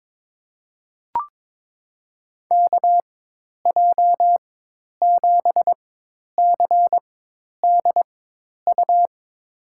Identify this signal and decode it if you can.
Morse code, heard as a keyed tone, about 22 wpm, spaced as for 10 wpm, 700 Hz. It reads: KJ7CDU
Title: USA Call Signs - 6 Characters - 4x Character Spacing - 22wpm